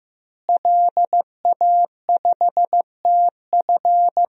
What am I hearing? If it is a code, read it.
Morse code, heard as a keyed tone, about 15 wpm, 700 Hz. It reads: LA5TF
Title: DXCC-100 Call Signs - 5 Characters - Repeated 2x - 15wpm